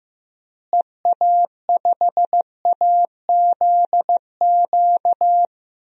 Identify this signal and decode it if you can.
Morse code, heard as a keyed tone, about 16 wpm, 700 Hz. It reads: EA5AZQ
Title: DXCC-100 Call Signs - 6 Characters - Repeated 3x - 15wpm